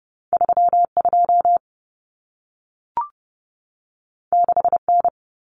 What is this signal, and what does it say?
Morse code, heard as a keyed tone, about 30 wpm, 700 Hz. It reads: 32 6N
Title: Common Cut Numbers - 2-Digit 30wpm